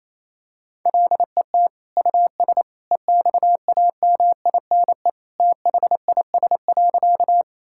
Morse code, heard as a keyed tone, about 28 wpm, 700 Hz. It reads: LET US EXAMINE THIS.
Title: Sentences from Top 1600 Words 28wpm